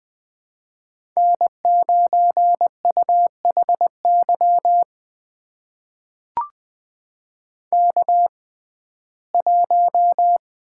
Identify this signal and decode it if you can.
Morse code, heard as a keyed tone, about 20 wpm, spaced as for 15 wpm, 700 Hz. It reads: N9UHY K1
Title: USA Call Signs - 5 Characters - 6x Character Spacing - 20wpm